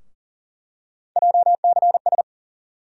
Morse code, heard as a keyed tone, about 40 wpm, 700 Hz. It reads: JCS
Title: USA Call Sign Postfixes - 3 Letters 40wpm